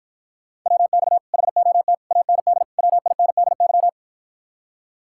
Morse code, heard as a keyed tone, about 54 wpm, 700 Hz. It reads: WX HOT AND WINDY